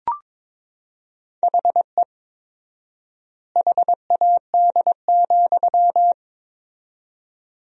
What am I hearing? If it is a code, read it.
Morse code, heard as a keyed tone, about 22 wpm, 700 Hz. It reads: HE HAD,
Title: 4x Word-Spacing - Sentences from Top 500 Words - 22wpm